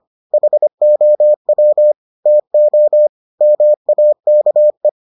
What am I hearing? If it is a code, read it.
Morse code, heard as a keyed tone, about 25 wpm, 600 Hz. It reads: HOW TO MAKE